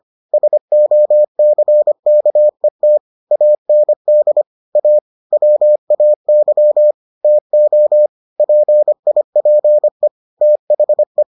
Morse code, heard as a keyed tone, about 25 wpm, 600 Hz. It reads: SOCKET AND A WAY TO PIPE THE